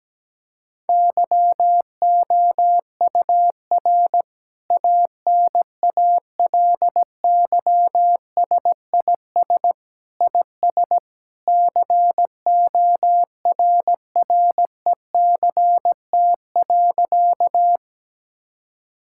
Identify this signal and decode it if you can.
Morse code, heard as a keyed tone, about 17 wpm, 700 Hz. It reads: YOUR ANALYSIS IS CORRECT.
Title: Sentences from Top 1000 Words - Review Part 5 - 17wpm